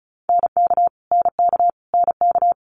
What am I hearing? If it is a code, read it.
Morse code, heard as a keyed tone, about 36 wpm, 700 Hz. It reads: NK NK NK
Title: Multi-letter Phonemes Repeated 3x 35wpm